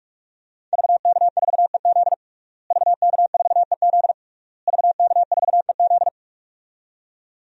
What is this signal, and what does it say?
Morse code, heard as a keyed tone, about 46 wpm, 700 Hz. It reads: VK4EZ VK4EZ VK4EZ
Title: DXCC-100 Call Signs - 5 Characters - Repeated 3x - 45wpm